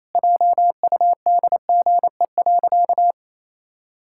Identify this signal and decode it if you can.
Morse code, heard as a keyed tone, about 27 wpm, 700 Hz. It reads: JUDGE.
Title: Pangram Challenge v2 - 28wpm